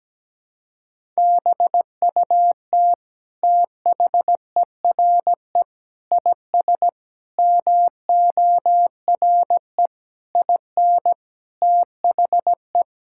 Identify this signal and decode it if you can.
Morse code, heard as a keyed tone, about 17 wpm, 700 Hz. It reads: BUT THERE IS MORE IN THE
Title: Sentences from Top 1000 Words - Review Part 5 - 17wpm